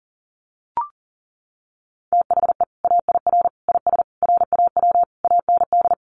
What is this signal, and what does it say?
Morse code, heard as a keyed tone, about 40 wpm, 700 Hz. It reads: THE AIR IS RAW AND
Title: Sentences from Top 1500 Words 40wpm